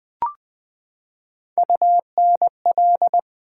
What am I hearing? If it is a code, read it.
Morse code, heard as a keyed tone, about 20 wpm, 700 Hz. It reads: UNL
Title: Build A Word - 3 Letters At A Time - 6-Letter Words - 20wpm